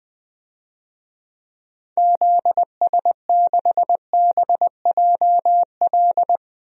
Morse code, heard as a keyed tone, about 20 wpm, 700 Hz. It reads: ZS6BJL
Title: DXCC-100 Call Signs - 6 Characters - 20wpm